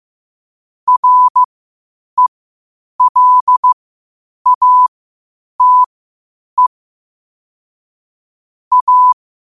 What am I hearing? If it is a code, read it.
Morse code, heard as a keyed tone, about 15 wpm, spaced as for 8 wpm, 1000 Hz. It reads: RELATE A
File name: qotdfiles / 8wpm